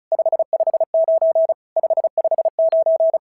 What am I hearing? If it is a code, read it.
Morse code, heard as a keyed tone, about 33 wpm, 650 Hz. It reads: 559 559
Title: SOTA QSO Practice 35wpm